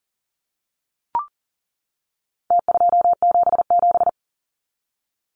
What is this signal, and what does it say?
Morse code, heard as a keyed tone, about 40 wpm, 700 Hz. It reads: T277
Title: Numbers - Contest Style - 4-Digit 40wpm